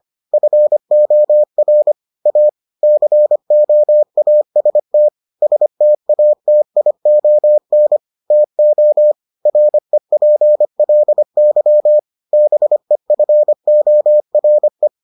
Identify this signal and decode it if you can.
Morse code, heard as a keyed tone, about 25 wpm, 600 Hz. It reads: FOR A COAST STATION TO REPLY BEFORE